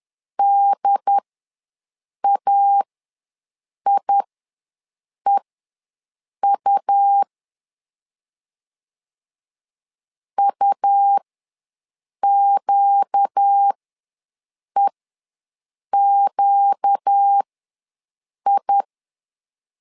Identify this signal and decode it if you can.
Morse code, heard as a keyed tone, about 10 wpm, spaced as for 6 wpm, 800 Hz. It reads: DAIEU UQEQI
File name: LEZIONI / 4 Q ws 10wpm 2306